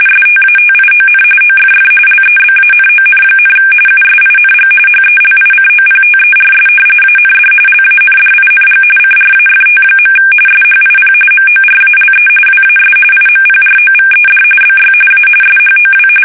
RTTY, about 45.45 baud. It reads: STATION OFF LEBANON, USING HELICOPTERS AND LANDING CRAFT. SHIP MORALE WAS IMPROVED BY A SHOW
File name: fsk850 / battleshipnj